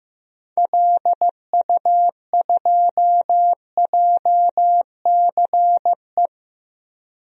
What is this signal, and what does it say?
Morse code, heard as a keyed tone, about 16 wpm, 700 Hz. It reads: LU2JCE
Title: DXCC-100 Call Signs - 6 Characters - 15wpm